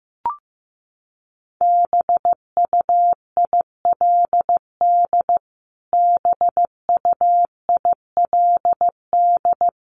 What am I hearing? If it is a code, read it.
Morse code, heard as a keyed tone, about 15 wpm, 700 Hz. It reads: BUILD BUILD
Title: Top 500 Words Repeated 15wpm